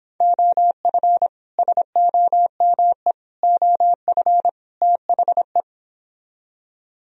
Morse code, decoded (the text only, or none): OF SOME OF THE